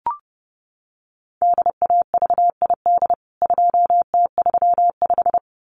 Morse code, heard as a keyed tone, about 28 wpm, 700 Hz. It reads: DAVID 2T35
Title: CWT Exchange 30wpm